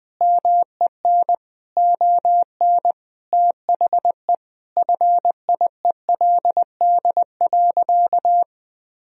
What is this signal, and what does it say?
Morse code, heard as a keyed tone, about 20 wpm, 700 Hz. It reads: MEN ON THE FIELD.
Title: Sentences from Top 1000 Words - Review Part 2 - 20wpm